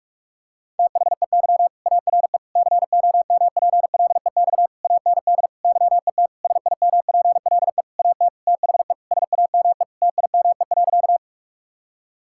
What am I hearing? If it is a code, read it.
Morse code, heard as a keyed tone, about 46 wpm, 700 Hz. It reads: THEY ARE COMPLEX AND YET SIMPLE AT THE SAME TIME.
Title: Sentences from Top 1000 Words - Review Part 6 - 45wpm